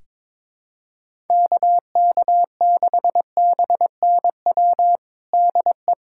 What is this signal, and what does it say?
Morse code, heard as a keyed tone, about 22 wpm, 700 Hz. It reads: KK6BNW DE
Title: USA Callsign DE Callsign 22wpm